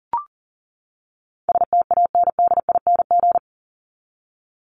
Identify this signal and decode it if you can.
Morse code, heard as a keyed tone, about 40 wpm, 700 Hz. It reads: STANDING
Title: ING Suffix - Top 100 Words with Base Words 40wpm